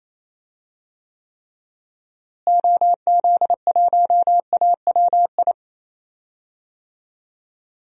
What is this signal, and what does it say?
Morse code, heard as a keyed tone, about 28 wpm, 700 Hz. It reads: OZ1AWI
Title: DXCC-100 Call Signs - 6 Characters - Repeated 3x - 28wpm